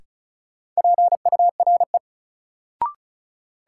Morse code, heard as a keyed tone, about 36 wpm, 700 Hz. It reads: PURE
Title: Sentences from Top 1500 Words - New Word Drill - 35wpm